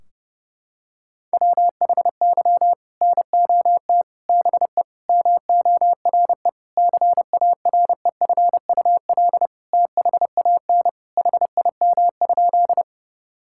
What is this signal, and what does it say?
Morse code, heard as a keyed tone, about 30 wpm, 700 Hz. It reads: WHY NOT BE MORE CAREFUL THAN HIM?
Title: Sentences from Top 1600 Words 30wpm